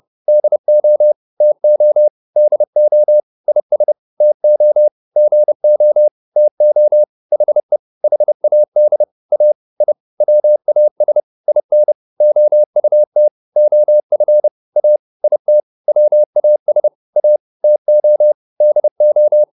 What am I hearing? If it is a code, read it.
Morse code, heard as a keyed tone, about 30 wpm, 600 Hz. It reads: DO TO DO IS TO GO TO HE HAD A I WAS IN OUT OF A IT WAS A TO DO
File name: WD201 30